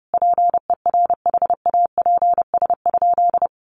Morse code, heard as a keyed tone, about 30 wpm, 700 Hz. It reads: PERHAPS?
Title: Sentences from Top 1200 Words 30wpm